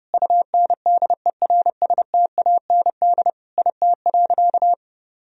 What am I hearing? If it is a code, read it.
Morse code, heard as a keyed tone, about 30 wpm, 700 Hz. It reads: UNDERSTAND IT.
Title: Sentences from Top 1000 Words - Review Part 3 - 30wpm